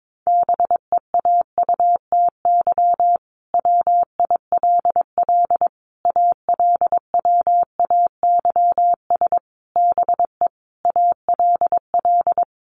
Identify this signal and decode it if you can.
Morse code, heard as a keyed tone, about 22 wpm, 700 Hz. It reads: BEAUTY WILL ALWAYS BE ALL
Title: Sentences from Top 1300 Words 22wpm